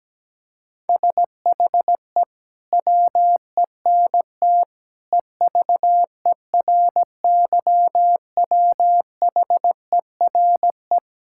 Morse code, heard as a keyed tone, about 17 wpm, 700 Hz. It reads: SHE WENT EVERYWHERE
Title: Sentences from Top 1600 Words 17wpm